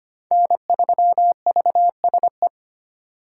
Morse code, heard as a keyed tone, about 25 wpm, 700 Hz. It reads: N3VSE
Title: USA Call Signs - 5 Characters - 4x Character Spacing - 25wpm